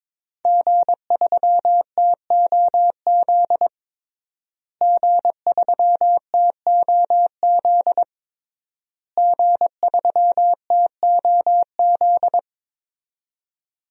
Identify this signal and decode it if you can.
Morse code, heard as a keyed tone, about 21 wpm, 700 Hz. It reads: G3TOZ G3TOZ G3TOZ
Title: DXCC-100 Call Signs - 5 Characters - Repeated 3x - 22wpm